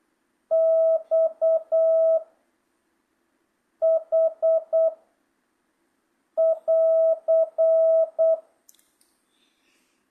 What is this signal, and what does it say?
Morse code, heard as a keyed tone, about 8 wpm, 1300 Hz. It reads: X H <AR>
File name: jh8cbh / 2025cw